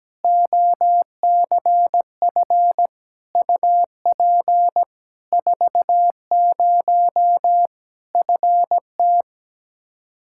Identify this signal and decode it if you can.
Morse code, heard as a keyed tone, about 17 wpm, 700 Hz. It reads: OCF UP 40 FT
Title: Practice QSO - Exchange 2 - Info - 17wpm